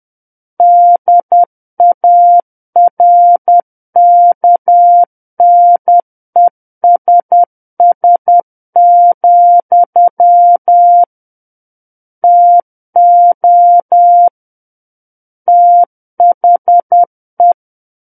Morse code, heard as a keyed tone, about 10 wpm, 700 Hz. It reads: DARKNESS, TO THE
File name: Book One / 01-Chapter 1 - 10 WPM